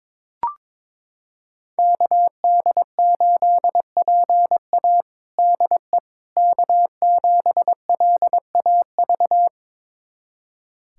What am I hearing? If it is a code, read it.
Morse code, heard as a keyed tone, about 22 wpm, 700 Hz. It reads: KD8PA DE K7LAV
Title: USA Callsign DE Callsign - 5 Characters - 22wpm